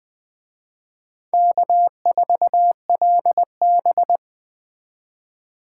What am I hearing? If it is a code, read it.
Morse code, heard as a keyed tone, about 21 wpm, 700 Hz. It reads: K4LB